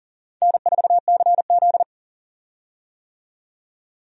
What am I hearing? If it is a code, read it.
Morse code, heard as a keyed tone, about 40 wpm, 700 Hz. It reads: N4CZ
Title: USA Call Signs - 4 Characters - 40wpm